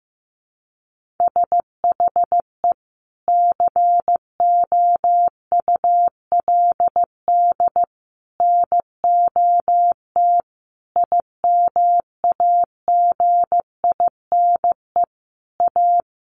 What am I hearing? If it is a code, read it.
Morse code, heard as a keyed tone, about 15 wpm, 700 Hz. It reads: SHE COULD NOT IMAGINE A
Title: Sentences from Top 1400 Words 15wpm